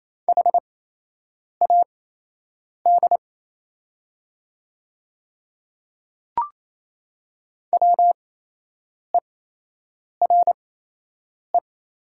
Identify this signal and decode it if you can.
Morse code, heard as a keyed tone, about 28 wpm, spaced as for 8 wpm, 700 Hz. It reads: HAD WERE
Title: 8x Character Spacing - Top 100 Words - 28wpm